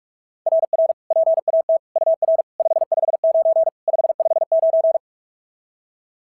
Morse code, heard as a keyed tone, about 44 wpm, 690 Hz. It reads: RR PAT UR 559 559
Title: SOTA QSO Practice 45wpm